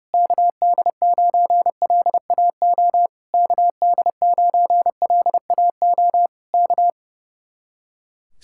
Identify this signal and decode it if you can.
Morse code, heard as a keyed tone, about 29 wpm, 700 Hz. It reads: KD9LAO KD9LAO K
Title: Speed-Racing Common QSO Phrases Encore - 30wpm